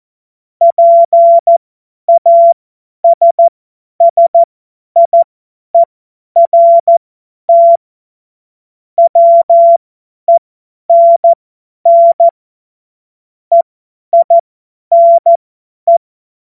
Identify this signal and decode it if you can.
Morse code, heard as a keyed tone, about 14 wpm, spaced as for 10 wpm, 680 Hz. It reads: PASSIERT WENN EINE
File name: Morsekurs / Witze / Witz 27. April 2021, 50 F70